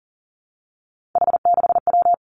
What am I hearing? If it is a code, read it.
Morse code, heard as a keyed tone, about 42 wpm, 700 Hz. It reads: H6W
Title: Three Challenging Letter-Number Combinations 40wpm